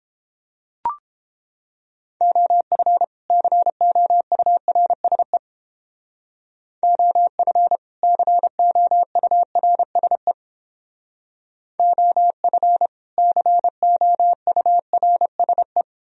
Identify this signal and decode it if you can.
Morse code, heard as a keyed tone, about 31 wpm, 700 Hz. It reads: OF COURSE OF COURSE OF COURSE